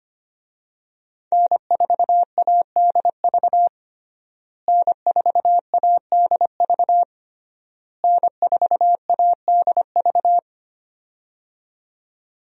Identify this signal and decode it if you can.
Morse code, heard as a keyed tone, about 25 wpm, 700 Hz. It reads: N4ADV N4ADV N4ADV
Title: USA Call Signs - 5 Characters - Repeated 3x - 25wpm